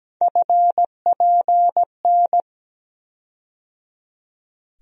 Morse code, heard as a keyed tone, about 17 wpm, 700 Hz. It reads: FPN
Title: USA Call Sign Postfixes - 3 Letters 17wpm